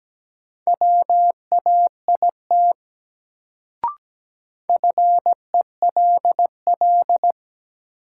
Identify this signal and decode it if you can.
Morse code, heard as a keyed tone, about 17 wpm, 700 Hz. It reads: WAIT FELL
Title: Sets of 5 Words Encore - New Word Drill - 17wpm